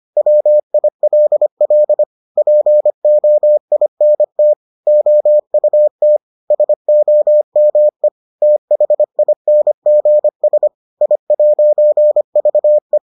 Morse code, heard as a keyed tone, about 25 wpm, 600 Hz. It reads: WILL POINT OUT SOME THINGS I'VE